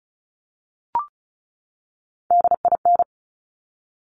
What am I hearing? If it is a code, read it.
Morse code, heard as a keyed tone, about 36 wpm, 700 Hz. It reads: DIN
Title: Top Three Letter Combinations in Words 35wpm